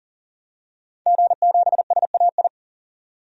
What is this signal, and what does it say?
Morse code, heard as a keyed tone, about 40 wpm, 700 Hz. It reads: G7SAI
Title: DXCC-100 Call Signs - 5 Characters - 40wpm